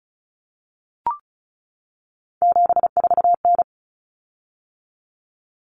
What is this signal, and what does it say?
Morse code, heard as a keyed tone, about 34 wpm, 700 Hz. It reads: 74N
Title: Common Cut Numbers - 3-Digit 35wpm